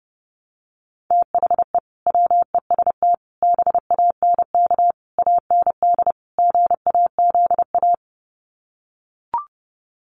Morse code, heard as a keyed tone, about 30 wpm, 700 Hz. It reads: THE WEST BANK AND GAZA
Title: Sets of 5 Words 30wpm